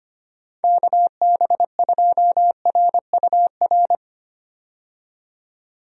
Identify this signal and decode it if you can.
Morse code, heard as a keyed tone, about 25 wpm, 700 Hz. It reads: KB2RUR
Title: USA Call Signs - 6 Characters - 25wpm